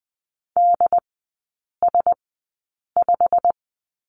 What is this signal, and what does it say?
Morse code, heard as a keyed tone, about 20 wpm, spaced as for 8 wpm, 700 Hz. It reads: DS5
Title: 087 Lesson 15e - Number 5 - Three Characters 20wpm